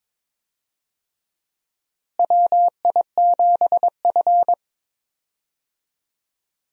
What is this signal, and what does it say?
Morse code, heard as a keyed tone, about 22 wpm, 700 Hz. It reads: WI7F
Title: USA Call Signs - 4 Characters - Repeated 3x - 22wpm